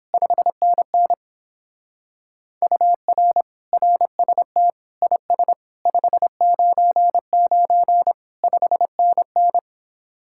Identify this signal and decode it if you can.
Morse code, heard as a keyed tone, about 29 wpm, 700 Hz. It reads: 5NN UR RST IS 599 5NN